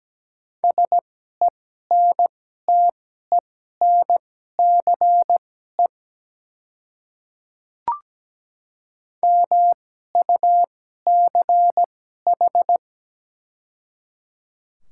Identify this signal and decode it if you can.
Morse code, heard as a keyed tone, about 17 wpm, spaced as for 12 wpm, 700 Hz. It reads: SENTENCE MUCH
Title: 2x Character Spacing - Top 300 Words - 17wpm